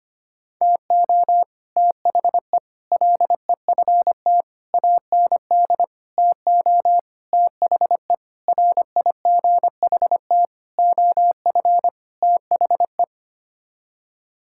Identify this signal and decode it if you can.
Morse code, heard as a keyed tone, about 25 wpm, 700 Hz. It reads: TO THE LEFT AND TO THE RIGHT OF THE
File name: Sets of 9 Words - 25wpm